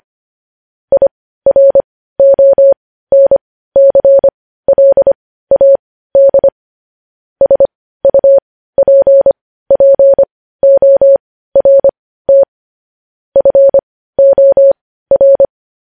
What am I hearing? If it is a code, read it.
Morse code, heard as a keyed tone, about 25 wpm, spaced as for 15 wpm, 550 Hz. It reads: IRONCLAD SUPPORT FOR
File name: podcast / News headlines-1x-25x15